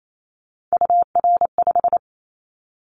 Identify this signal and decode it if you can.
Morse code, heard as a keyed tone, about 28 wpm, 700 Hz. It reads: UR5
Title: Callsign Prefixes for Top 25 DXCC - Repeated 3x - 28wpm